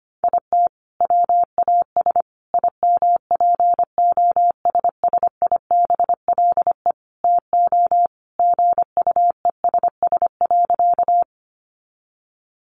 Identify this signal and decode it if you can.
Morse code, heard as a keyed tone, about 25 wpm, 700 Hz. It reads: IT WAS IMPOSSIBLE TO GUESS.
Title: Sentences from Top 1500 Words 25wpm